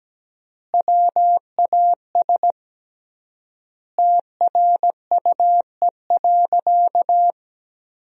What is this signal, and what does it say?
Morse code, heard as a keyed tone, about 17 wpm, 700 Hz. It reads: WAS TRUE.
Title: 3x Word-Spacing - Sentences from Top 400 Words - 17wpm